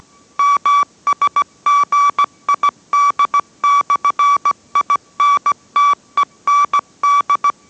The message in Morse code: MSGID/INTEND

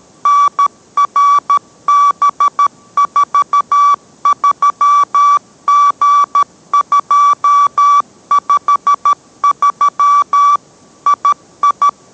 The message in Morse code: NRB43G253 II

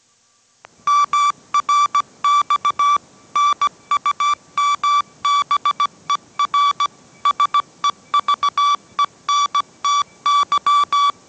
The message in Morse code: MRX NUMBER SEVENTY

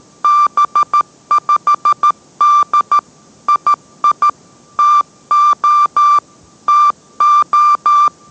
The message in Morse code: B5D II TO TO

